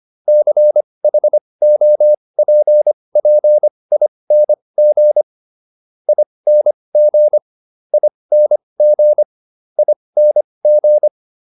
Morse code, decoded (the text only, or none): CHOPPING ING ING ING